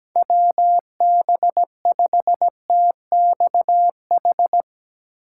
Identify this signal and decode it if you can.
Morse code, heard as a keyed tone, about 17 wpm, 700 Hz. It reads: WB5TXH
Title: USA Call Signs - 6 Characters - 8x Character Spacing - 17wpm